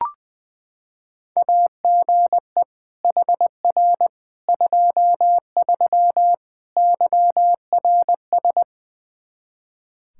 Morse code, heard as a keyed tone, about 21 wpm, 700 Hz. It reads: AGE HR 23 YRS
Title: Practice QSO - Exchange 3 - More Info - 20wpm